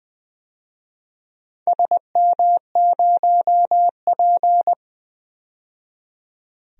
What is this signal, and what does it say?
Morse code, heard as a keyed tone, about 20 wpm, 700 Hz. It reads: SM0P
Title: DXCC-100 Call Signs - 4 Characters - 20wpm